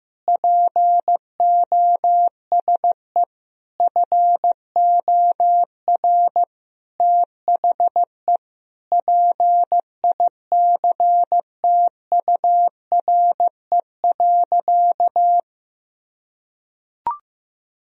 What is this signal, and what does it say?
Morse code, heard as a keyed tone, about 15 wpm, 700 Hz. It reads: POSE FOR THE PICTURE.
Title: Sentences from Top 1000 Words - Review Part 2 - 15wpm